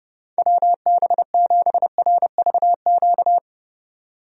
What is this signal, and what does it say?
Morse code, heard as a keyed tone, about 30 wpm, 700 Hz. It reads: WB7RVQ
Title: USA Call Signs - 6 Characters - 30wpm